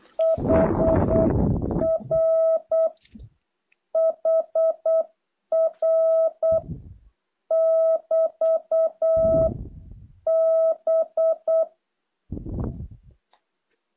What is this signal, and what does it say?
Morse code, heard as a keyed tone, about 8 wpm, 650 Hz. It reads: HR HR <BT> B
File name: jh8cbh / 2023cw